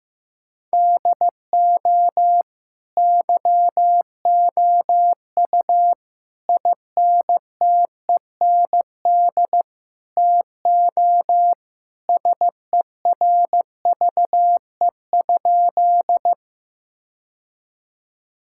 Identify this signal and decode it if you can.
Morse code, heard as a keyed tone, about 15 wpm, 700 Hz. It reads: DO YOU INTEND TO SERVE?
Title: Sentences from Top 1600 Words 15wpm